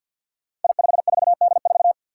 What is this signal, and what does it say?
Morse code, heard as a keyed tone, about 46 wpm, 700 Hz. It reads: I54D4
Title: Five Challenging Letter-Number Combinations 50wpm